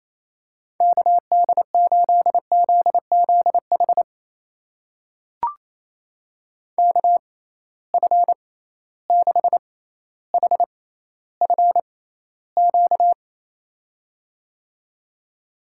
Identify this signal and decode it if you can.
Morse code, heard as a keyed tone, about 28 wpm, spaced as for 14 wpm, 700 Hz. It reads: KD8ZZH KF6HFQ
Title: USA Call Signs - 6 Characters - 6x Character Spacing - 28wpm